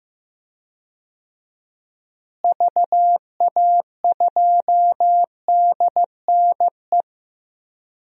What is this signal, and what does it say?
Morse code, heard as a keyed tone, about 16 wpm, 700 Hz. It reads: VA2DNE
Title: DXCC-100 Call Signs - 6 Characters - Repeated 2x - 15wpm